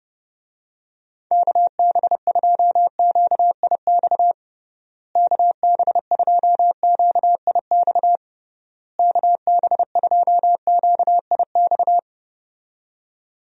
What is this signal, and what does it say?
Morse code, heard as a keyed tone, about 30 wpm, 700 Hz. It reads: KB2QIX KB2QIX KB2QIX
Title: USA Call Signs - 6 Characters - Repeated 3x - 30wpm